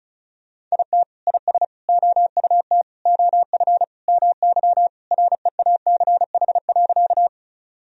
Morse code, heard as a keyed tone, about 34 wpm, 700 Hz. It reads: IT IS OUT OF MY REACH.
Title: Sentences from Top 1000 Words - Review Part 2 - 35wpm